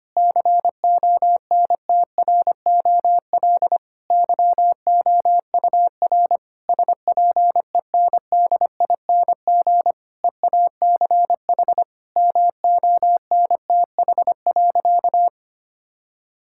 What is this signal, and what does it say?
Morse code, heard as a keyed tone, about 25 wpm, 700 Hz. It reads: CONTROL YOUR SPENDING EACH MONTH.
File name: practice / 25wpm / Difficult Sentences with Hints - 25wpm